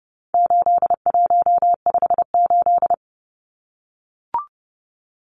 Morse code, heard as a keyed tone, about 30 wpm, 700 Hz. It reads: 8158
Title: Numbers - 4-Digit 30wpm